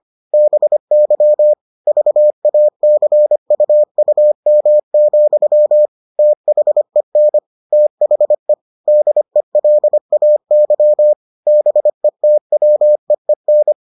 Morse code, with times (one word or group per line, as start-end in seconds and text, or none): 0.3-1.5 BY
1.9-5.9 VACUUM,
6.2-7.4 THEN
7.7-8.5 THE
8.9-11.1 DELAY
11.5-13.7 BETWEEN